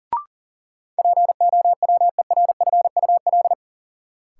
Morse code, heard as a keyed tone, about 40 wpm, 700 Hz. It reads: POWERFUL
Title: Sentences from Top 1000 Words - Review Part 6 - New Word Drill - 40wpm